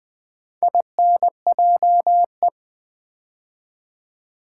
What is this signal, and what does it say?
Morse code, heard as a keyed tone, about 20 wpm, 700 Hz. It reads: INJE
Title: Build A Word - 2 Letters At A Time - 6-Letter Words - 20wpm